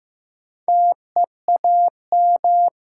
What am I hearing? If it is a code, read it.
Morse code, heard as a keyed tone, about 15 wpm, 700 Hz. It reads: TEAM
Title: Sentences from Top 1500 Words - New Word Drill - 15wpm